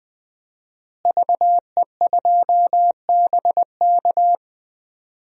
Morse code, heard as a keyed tone, about 20 wpm, 700 Hz. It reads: VE2BK
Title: DXCC-100 Call Signs - 5 Characters - Repeated 3x - 20wpm